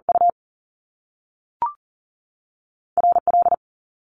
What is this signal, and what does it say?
Morse code, heard as a keyed tone, about 38 wpm, spaced as for 3 wpm, 700 Hz. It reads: UR L
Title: Top Two Letter Combinations in Words 40wpm